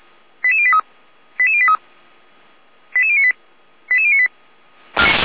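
Five-tone selective call (ZVEI), heard as quick runs of five tones, tones of about 70 ms. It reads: 88083 88083 88098 88098
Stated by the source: Five-tone ZVEI selective call received by SDR, from audio server of Feuerwehr Orlamünde.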